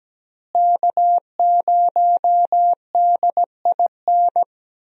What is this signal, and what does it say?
Morse code, heard as a keyed tone, about 17 wpm, 700 Hz. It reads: K0DIN
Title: USA Call Signs - 5 Characters - Repeated 3x - 17wpm